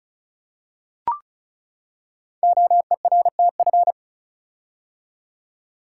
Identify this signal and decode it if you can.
Morse code, heard as a keyed tone, about 36 wpm, 690 Hz. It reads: OERTF
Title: Code Groups - 5 Letters - 35wpm